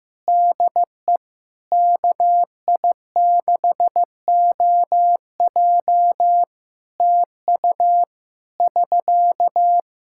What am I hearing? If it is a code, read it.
Morse code, heard as a keyed tone, about 15 wpm, 700 Hz. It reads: DE KI6OJ TU <SK>
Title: K1USN SST QSO Practice 15wpm